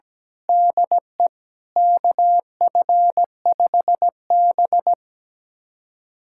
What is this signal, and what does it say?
Morse code, heard as a keyed tone, about 17 wpm, 700 Hz. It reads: DE KF5B
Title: USA Callsign DE Callsign - 4 Characters - 17wpm